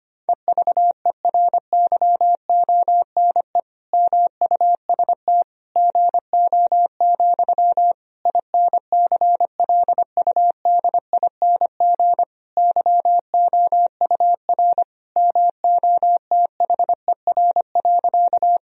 Morse code, heard as a keyed tone, about 25 wpm, 700 Hz. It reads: EVERYONE MUST GO, INCLUDING YOUR MOTHER.
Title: Sentences from Top 1000 Words - Review Part 4 - 25wpm